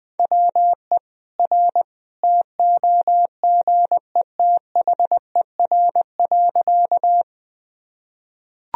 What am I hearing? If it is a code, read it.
Morse code, heard as a keyed tone, about 21 wpm, 700 Hz. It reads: WE R TOGETHER.
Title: Sentences from Top 300 Words and Abbreviations 20wpm